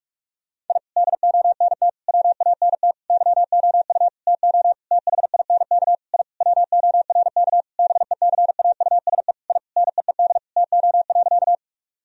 Morse code, heard as a keyed tone, about 46 wpm, 700 Hz. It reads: I DONT WANT YOU TO THINK I WORK BECAUSE I NEED TO.
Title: Sentences from Top 1000 Words - Review Part 3 - 45wpm